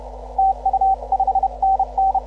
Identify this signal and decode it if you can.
Morse code, heard as a keyed tone, about 32 wpm, 750 Hz, with radio noise. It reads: TU 5NN